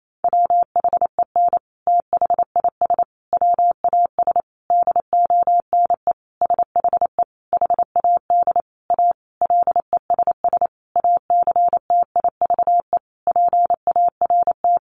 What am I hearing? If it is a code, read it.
Morse code, heard as a keyed tone, about 28 wpm, 700 Hz. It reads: WHEN THIS WAS DONE SHE HAD A LESS ACTIVE PART
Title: Sentences from Top 900 Words 28wpm